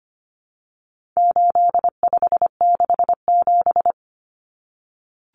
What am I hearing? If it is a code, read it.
Morse code, heard as a keyed tone, about 25 wpm, 700 Hz. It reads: 8567
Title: Numbers - 4-Digit 25wpm